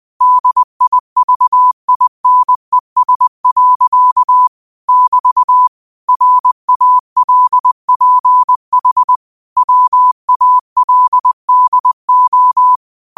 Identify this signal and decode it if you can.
Morse code, heard as a keyed tone, about 20 wpm, 1000 Hz. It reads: DIVINES. <BT> RALPH WALDO
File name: qotdfiles / 20wpm